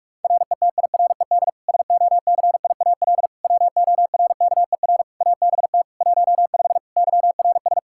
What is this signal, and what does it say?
Morse code, heard as a keyed tone, about 46 wpm, 700 Hz. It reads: RETIRED SOCIAL WORKER ABT 15 YRS
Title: Practice QSO - Exchange 3 - More Info - 45wpm